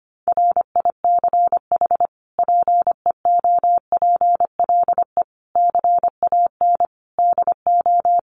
Morse code, heard as a keyed tone, about 25 wpm, 700 Hz. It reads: RICH PEOPLE CAN DO
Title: Sentences from Top 1100 Words 25wpm